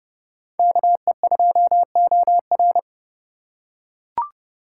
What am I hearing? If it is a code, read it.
Morse code, heard as a keyed tone, about 28 wpm, 700 Hz. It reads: KE2OR